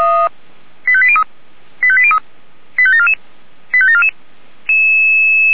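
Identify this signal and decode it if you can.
Five-tone selective call (ZVEI), heard as quick runs of five tones, tones of about 70 ms. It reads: 85803 85803 85744 85744
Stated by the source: ZVEI five-tone selective call received on an SDR.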